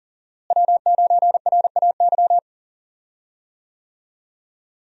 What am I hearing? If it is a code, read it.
Morse code, heard as a keyed tone, about 39 wpm, 700 Hz. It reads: W9RAY
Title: USA Call Signs - 5 Characters - Repeated 2x - 40wpm